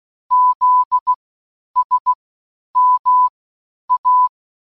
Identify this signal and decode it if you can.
Morse code, heard as a keyed tone, about 16 wpm, spaced as for 9 wpm, 1000 Hz. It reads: ZSMA